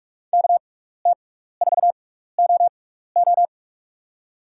Morse code, heard as a keyed tone, about 44 wpm, spaced as for 15 wpm, 690 Hz. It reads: KT4OO
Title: USA Call Signs - 5 Characters - 6x Character Spacing - 45wpm